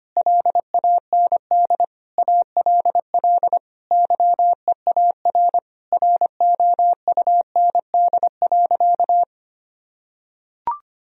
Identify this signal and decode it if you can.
Morse code, heard as a keyed tone, about 25 wpm, 700 Hz. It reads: LAND ALL YEAR ROUND.
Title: Sentences from Top 1000 Words - Review Part 1 - 25wpm